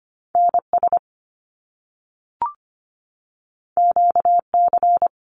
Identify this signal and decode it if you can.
Morse code, heard as a keyed tone, about 25 wpm, 700 Hz. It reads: NS QC
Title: Canadian Provinces and Territories 25wpm